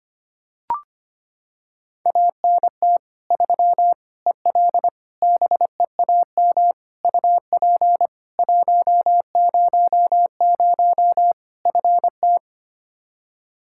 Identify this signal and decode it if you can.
Morse code, heard as a keyed tone, about 25 wpm, 700 Hz. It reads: ANT 3 EL BEAM UP 100 FT
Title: K1USN SST QSO Practice 25wpm